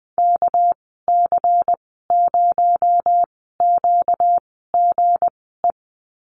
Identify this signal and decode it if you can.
Morse code, heard as a keyed tone, about 20 wpm, spaced as for 14 wpm, 700 Hz. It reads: KC0QGE
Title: 238 Lesson 38g - Letter q - Call Signs 20wpm